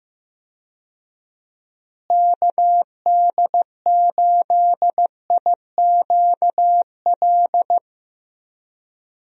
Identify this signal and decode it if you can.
Morse code, heard as a keyed tone, about 15 wpm, 700 Hz. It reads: KD8IQL
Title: USA Call Signs - 6 Characters - Repeated 3x - 15wpm